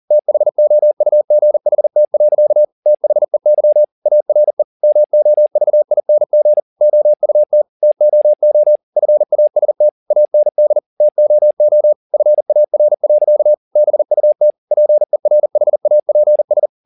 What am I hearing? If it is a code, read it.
Morse code, heard as a keyed tone, about 40 wpm, 600 Hz. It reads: THOUGHT. THEY ARE MOVING OUT TOO FAST AND TOO FAR. BUT PERHAPS